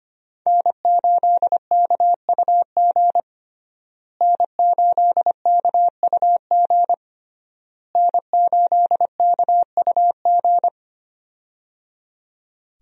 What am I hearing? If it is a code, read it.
Morse code, heard as a keyed tone, about 25 wpm, 700 Hz. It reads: N8KUG N8KUG N8KUG
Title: USA Call Signs - 5 Characters - Repeated 3x - 25wpm